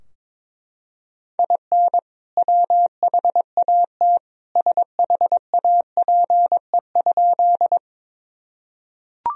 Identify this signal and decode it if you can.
Morse code, heard as a keyed tone, about 22 wpm, 700 Hz. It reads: IN WHAT SHAPE?
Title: Sentences from Top 1000 Words - Review Part 3 - 22wpm